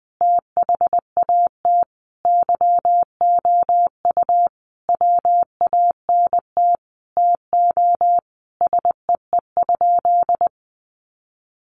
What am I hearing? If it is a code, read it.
Morse code, heard as a keyed tone, about 20 wpm, 700 Hz. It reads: THAT YOU WANT TO SEE?
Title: Sentences from Top 1100 Words 20wpm